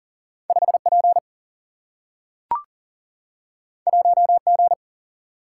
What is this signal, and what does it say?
Morse code, heard as a keyed tone, about 39 wpm, 700 Hz. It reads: 5P 1G
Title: Single-Letter Single-Number 40wpm